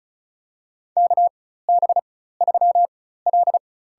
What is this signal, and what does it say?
Morse code, heard as a keyed tone, about 36 wpm, spaced as for 16 wpm, 700 Hz. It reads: KB3L
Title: USA Call Signs - 4x Character Spacing - 35wpm